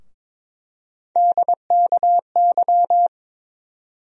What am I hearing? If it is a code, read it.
Morse code, heard as a keyed tone, about 22 wpm, 700 Hz. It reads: DKY
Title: USA Call Sign Postfixes - 3 Letters 22wpm